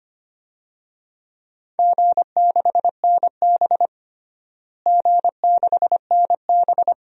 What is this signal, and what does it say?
Morse code, heard as a keyed tone, about 25 wpm, 700 Hz. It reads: G6NB G6NB
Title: DXCC-100 Call Signs - 4 Characters - Repeated 2x - 25wpm